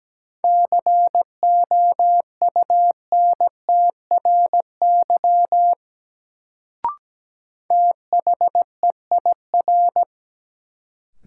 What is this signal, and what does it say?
Morse code, heard as a keyed tone, about 17 wpm, 700 Hz. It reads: COUNTRY THEIR
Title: Sentences from Top 300 Words - New Word Drill - 17wpm